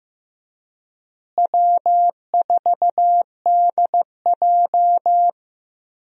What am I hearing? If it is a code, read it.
Morse code, heard as a keyed tone, about 15 wpm, 700 Hz. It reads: W4DJ
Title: USA Call Signs - 4 Characters - Repeated 3x - 15wpm